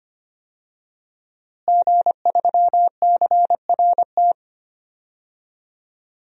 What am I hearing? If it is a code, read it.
Morse code, heard as a keyed tone, about 25 wpm, 700 Hz. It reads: G3CRT
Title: DXCC-100 Call Signs - 5 Characters - Repeated 2x - 25wpm